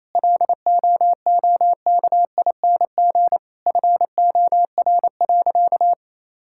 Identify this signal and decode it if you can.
Morse code, heard as a keyed tone, about 28 wpm, 700 Hz. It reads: LOOKING FOR.
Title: Sentences from Top 1000 Words - Review Part 6 - 28wpm